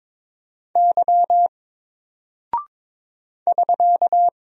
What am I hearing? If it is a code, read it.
Morse code, heard as a keyed tone, about 22 wpm, 700 Hz. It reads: Y <SK>
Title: Single Character (Letter:Number:Punctuation:Prosign) 22wpm